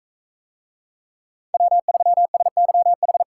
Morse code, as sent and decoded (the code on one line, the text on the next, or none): .-- ...-- ... -.-- ....
W3SYH